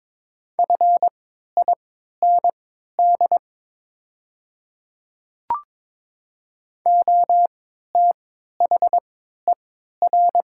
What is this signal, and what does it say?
Morse code, heard as a keyed tone, about 22 wpm, spaced as for 12 wpm, 700 Hz. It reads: FIND OTHER
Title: 3x Character Spacing - Top 100 Words - 22wpm